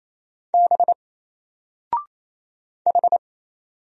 Morse code, heard as a keyed tone, about 27 wpm, 700 Hz. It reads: B H
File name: Single-Letter-Number-28wpm